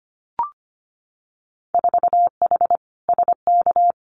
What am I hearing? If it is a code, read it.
Morse code, heard as a keyed tone, about 25 wpm, 700 Hz. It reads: 4H SK
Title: Winter Field Day Exchanges Encore 25wpm